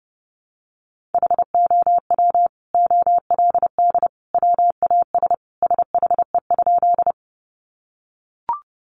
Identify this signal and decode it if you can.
Morse code, heard as a keyed tone, about 30 wpm, 700 Hz. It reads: HOW OLD WAS SHE?
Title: Sentences from Top 200 Words 30wpm